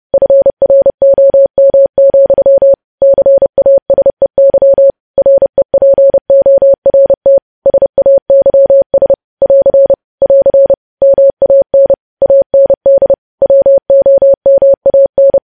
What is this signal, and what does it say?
Morse code, heard as a keyed tone, about 30 wpm, 550 Hz. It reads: FROM, CASEY REPORT SAYS <AR> <AR> MAN AND WOMAN
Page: news headlines in morse code at 30 wpm